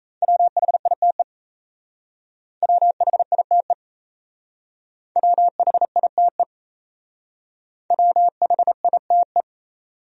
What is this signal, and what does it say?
Morse code, heard as a keyed tone, about 33 wpm, 700 Hz. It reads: WHITE WHITE WHITE WHITE
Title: Speed-Racing Top 500 Words 28wpm